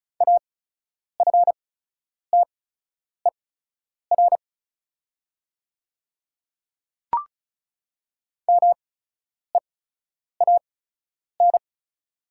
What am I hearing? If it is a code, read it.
Morse code, heard as a keyed tone, about 36 wpm, spaced as for 10 wpm, 690 Hz. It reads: AFTER MEAN
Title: 8x Character Spacing - Top 200 Words - 35wpm